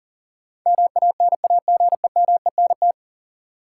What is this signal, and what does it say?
Morse code, heard as a keyed tone, about 40 wpm, 700 Hz. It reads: MANAGEMENT
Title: Sentences from Top 1200 Words - New Word Drill Alt - 40wpm